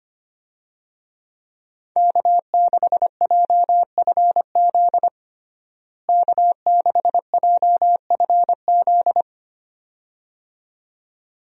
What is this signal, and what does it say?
Morse code, heard as a keyed tone, about 25 wpm, 700 Hz. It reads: K6JFZ K6JFZ
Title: USA Call Signs - 5 Characters - Repeated 2x - 25wpm